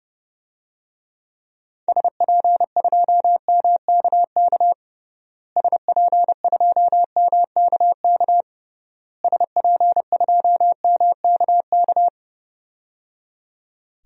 Morse code, heard as a keyed tone, about 30 wpm, 700 Hz. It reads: SP2MKK SP2MKK SP2MKK
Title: DXCC-100 Call Signs - 6 Characters - Repeated 3x - 30wpm